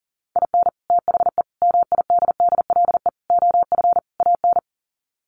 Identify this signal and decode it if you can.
Morse code, heard as a keyed tone, about 40 wpm, 700 Hz. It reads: IN THE MIDDLE OF AN
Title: Sets of 5 Words Encore2 40wpm